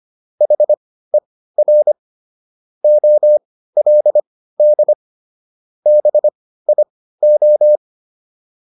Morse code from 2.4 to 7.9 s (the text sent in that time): OLD BIO